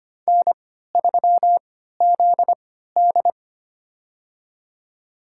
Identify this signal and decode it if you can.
Morse code, heard as a keyed tone, about 25 wpm, spaced as for 14 wpm, 700 Hz. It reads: N3ZD
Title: USA Call Signs - 3x Character Spacing - 25wpm